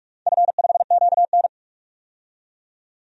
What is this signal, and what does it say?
Morse code, heard as a keyed tone, about 46 wpm, 700 Hz. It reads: F5QN
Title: DXCC-100 Call Signs - 4 Characters - 45wpm